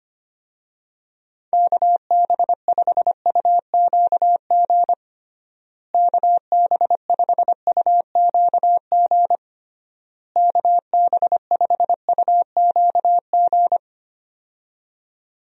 Morse code, heard as a keyed tone, about 25 wpm, 700 Hz. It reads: KB5UQG KB5UQG KB5UQG
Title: USA Call Signs - 6 Characters - Repeated 3x - 25wpm